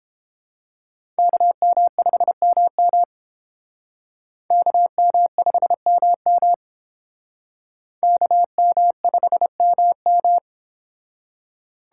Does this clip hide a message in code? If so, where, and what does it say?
Yes: Morse code from 1.2 to 10.4 s, heard as a keyed tone, about 31 wpm, 700 Hz. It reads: KM5MM KM5MM KM5MM